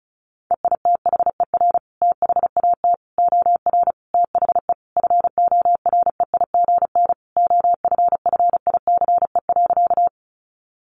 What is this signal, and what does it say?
Morse code, heard as a keyed tone, about 34 wpm, 700 Hz. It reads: EITHER THAT OR THE FOREIGN OFFICE.
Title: Sentences from Top 1400 Words 35wpm